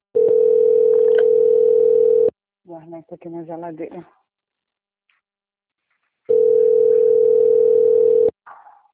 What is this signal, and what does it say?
Ringback tone (North America), 440 + 480 Hz, with speech.
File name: out- - - - - .8717